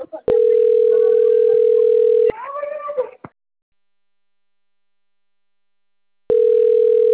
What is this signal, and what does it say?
Ringback tone (North America), 440 + 480 Hz.